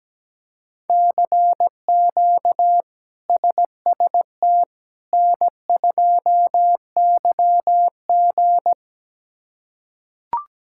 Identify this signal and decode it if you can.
Morse code, heard as a keyed tone, about 17 wpm, 700 Hz. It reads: CQ SST N2YG
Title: K1USN SST QSO Practice - 17wpm